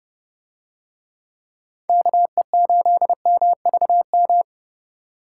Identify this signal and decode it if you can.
Morse code, heard as a keyed tone, about 29 wpm, 700 Hz. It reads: KE8MVM